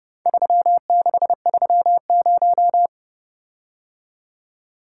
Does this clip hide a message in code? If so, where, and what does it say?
Yes: Morse code from 0.3 to 2.9 s, heard as a keyed tone, about 30 wpm, 700 Hz. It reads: 3630